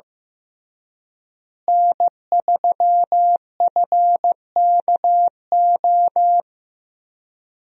Morse code, heard as a keyed tone, about 15 wpm, 700 Hz. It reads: N3FKO
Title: USA Call Signs - 5 Characters - Repeated 2x - 15wpm